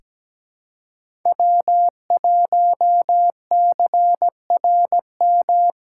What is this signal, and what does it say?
Morse code, heard as a keyed tone, about 17 wpm, 700 Hz. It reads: W1CRM
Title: USA Call Signs - 5 Characters - 2x Character Spacing - 17wpm